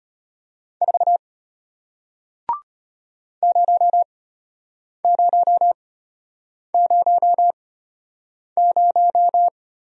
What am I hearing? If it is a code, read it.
Morse code, heard as a keyed tone, about 33 wpm, spaced as for 8 wpm, 700 Hz. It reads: 4 0000